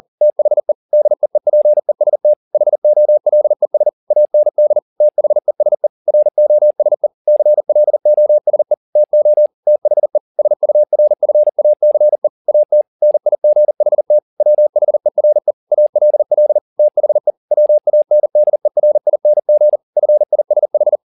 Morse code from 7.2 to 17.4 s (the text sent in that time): CLOSE TO THE SURFACE AT NIGHT WHERE ALL THE